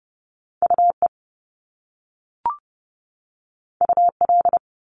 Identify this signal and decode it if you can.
Morse code, heard as a keyed tone, about 30 wpm, 700 Hz. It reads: UE UL
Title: Top Two Letter Combinations in Words 30wpm